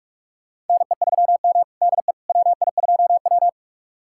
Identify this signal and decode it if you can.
Morse code, heard as a keyed tone, about 46 wpm, 700 Hz. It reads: NE3M DE WI2W
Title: USA Callsign DE Callsign - 4 Characters - 45wpm